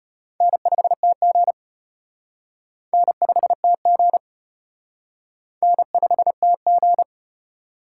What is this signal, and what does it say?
Morse code, heard as a keyed tone, about 34 wpm, 700 Hz. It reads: N5TG N5TG N5TG